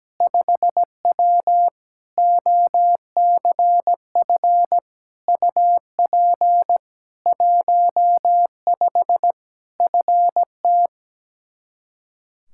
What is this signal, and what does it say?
Morse code, heard as a keyed tone, about 17 wpm, 700 Hz. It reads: 5W OCF UP 15 FT
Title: K1USN SST QSO Practice 17wpm